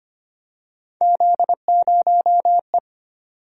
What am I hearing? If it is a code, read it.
Morse code, heard as a keyed tone, about 25 wpm, 700 Hz. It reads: Z0E
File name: Three Letter-Number Combinations - 25wpm